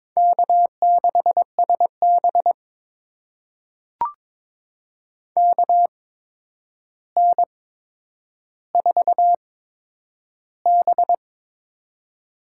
Morse code, heard as a keyed tone, about 22 wpm, spaced as for 9 wpm, 700 Hz. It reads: K6SB KN4B
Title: USA Call Signs - 4 Characters - 8x Character Spacing - 22wpm